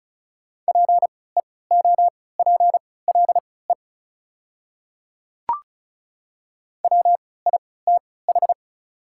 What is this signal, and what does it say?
Morse code, heard as a keyed tone, about 36 wpm, spaced as for 20 wpm, 690 Hz. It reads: PEOPLE WITH
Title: 3x Character Spacing - Top 100 Words - 35wpm